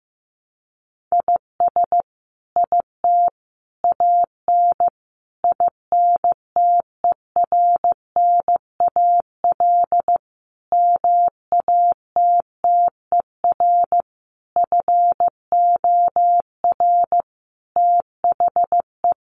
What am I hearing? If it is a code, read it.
Morse code, heard as a keyed tone, about 15 wpm, 700 Hz. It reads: IS IT AN INTERNAL MATTER FOR THE
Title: Sentences from Top 1400 Words 15wpm